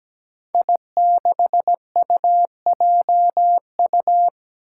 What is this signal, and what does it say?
Morse code, heard as a keyed tone, about 17 wpm, 700 Hz. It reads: I6UJU
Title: Five Challenging Letter-Number Combinations 17wpm